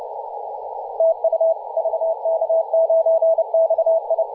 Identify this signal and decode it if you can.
Morse code, heard as a keyed tone, about 30 wpm, 650 Hz, with radio noise. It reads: TU VK9XI
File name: posts / 2017-oct-04-0559z-vk9xi-21-032mhz-cw